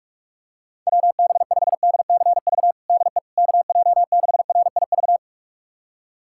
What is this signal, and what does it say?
Morse code, heard as a keyed tone, about 41 wpm, 700 Hz. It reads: WB5DCV DE KJ6RIV
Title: USA Callsign DE Callsign - 6 Characters - 45wpm